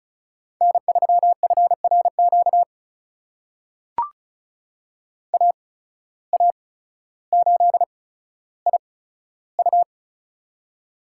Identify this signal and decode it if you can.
Morse code, heard as a keyed tone, about 34 wpm, spaced as for 15 wpm, 700 Hz. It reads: N3FRQ AA8IU
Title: USA Call Signs - 5 Characters - 8x Character Spacing - 35wpm